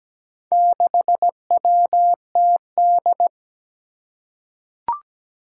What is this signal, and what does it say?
Morse code, heard as a keyed tone, about 17 wpm, 700 Hz. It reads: 6WTD
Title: Four Challenging Letter-Number Combinations 17wpm